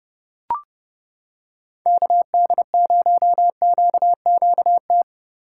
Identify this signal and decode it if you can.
Morse code, heard as a keyed tone, about 29 wpm, 700 Hz. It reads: KD0QQT